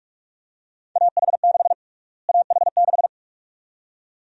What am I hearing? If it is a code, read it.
Morse code, heard as a keyed tone, about 42 wpm, 700 Hz. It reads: AH6 AH6
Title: USA Call Sign Prefixes - Repeated 2x - 45wpm